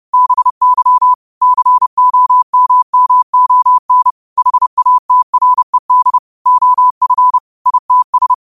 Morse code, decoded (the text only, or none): BY COMMON HATRED OF ITS